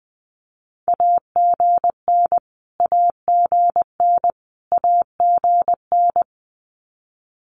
Morse code, heard as a keyed tone, about 20 wpm, 700 Hz. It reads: AGN AGN AGN
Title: Abbreviations Repeated 3x - 20wpm